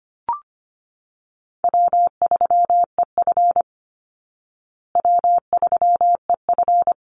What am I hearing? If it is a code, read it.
Morse code, heard as a keyed tone, about 25 wpm, 700 Hz. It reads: W3EF W3EF
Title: Call Signs Repeated 3x - 25wpm